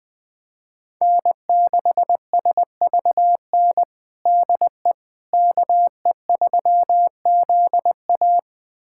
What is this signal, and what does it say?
Morse code, heard as a keyed tone, about 20 wpm, 700 Hz. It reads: N6SVN DE KE3ZA
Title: USA Callsign DE Callsign - 5 Characters - 20wpm